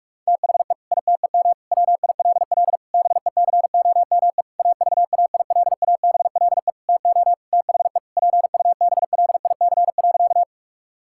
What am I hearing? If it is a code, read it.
Morse code, heard as a keyed tone, about 46 wpm, 700 Hz. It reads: THE ITEM WILL BECOME AVAILABLE TO THE PUBLIC.
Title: Sentences from Top 1000 Words - Review Part 6 - 45wpm